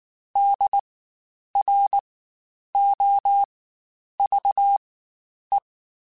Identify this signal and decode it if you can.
Morse code, heard as a keyed tone, about 19 wpm, spaced as for 9 wpm, 800 Hz. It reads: DROVE
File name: Spaced Alphas / Spaced Alphas mp3s / Ds sp @18